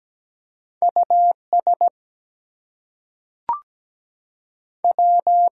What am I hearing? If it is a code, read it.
Morse code, heard as a keyed tone, about 17 wpm, 700 Hz. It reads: US W
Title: Two Letter Words - 17wpm